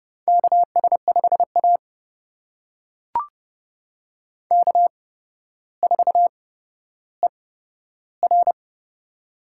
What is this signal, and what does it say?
Morse code, heard as a keyed tone, about 30 wpm, spaced as for 13 wpm, 700 Hz. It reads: KS5A K4ER
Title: USA Call Signs - 4 Characters - 8x Character Spacing - 30wpm